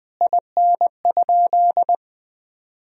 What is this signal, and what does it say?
Morse code, heard as a keyed tone, about 20 wpm, 700 Hz. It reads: IN?
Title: Sentences from Top 1000 Words - Review Part 3 - 20wpm